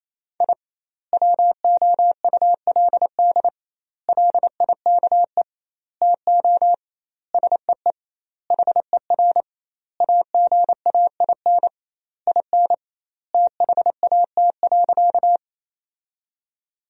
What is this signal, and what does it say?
Morse code, heard as a keyed tone, about 28 wpm, 700 Hz. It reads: I WOULD LIKE TO SEE HER AGAIN IN THAT.
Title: 2x Word-Spacing - Sentences from Top 200 Words - 28wpm